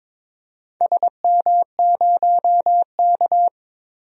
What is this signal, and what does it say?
Morse code, heard as a keyed tone, about 22 wpm, 700 Hz. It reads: SM0K
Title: DXCC-100 Call Signs - 4 Characters - 22wpm